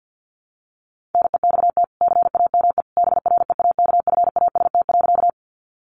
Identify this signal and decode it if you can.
Morse code, heard as a keyed tone, about 50 wpm, 700 Hz. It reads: NEXT CAME BREAKFAST.
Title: Sentences from Top 1500 Words 50wpm